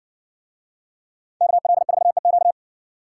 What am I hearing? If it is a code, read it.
Morse code, heard as a keyed tone, about 60 wpm, 700 Hz. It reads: KL4EQ